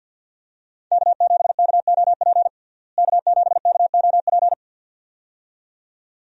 Morse code, heard as a keyed tone, about 53 wpm, 700 Hz. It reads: K7KOP K7KOP
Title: USA Call Signs - 5 Characters - Repeated 2x - 50wpm